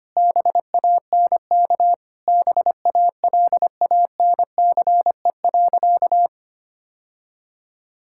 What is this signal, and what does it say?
Morse code, heard as a keyed tone, about 25 wpm, 700 Hz. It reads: BANK BALANCE.
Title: Sentences from Top 1000 Words - Review Part 6 - 25wpm